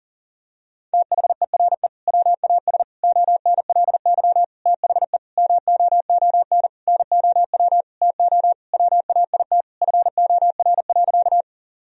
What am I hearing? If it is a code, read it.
Morse code, heard as a keyed tone, about 40 wpm, 700 Hz. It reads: THERE WAS ONLY THE MOON NOW TO WAIT FOR.
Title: Sentences from Top 1000 Words - Review Part 3 - 40wpm